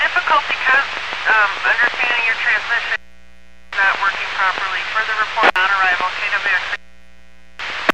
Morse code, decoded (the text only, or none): none